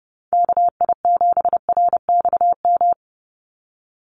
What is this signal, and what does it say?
Morse code, heard as a keyed tone, about 30 wpm, 700 Hz. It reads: KI7RXM
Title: USA Call Signs 30wpm